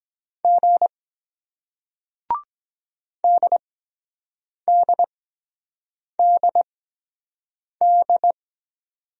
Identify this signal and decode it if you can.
Morse code, heard as a keyed tone, about 22 wpm, spaced as for 7 wpm, 700 Hz. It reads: G DDDD